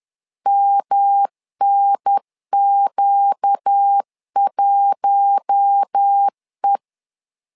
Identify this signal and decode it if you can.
Morse code, heard as a keyed tone, about 10 wpm, 800 Hz. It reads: MNQ1E
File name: LEZIONI / 4 Q1 10wpm 2448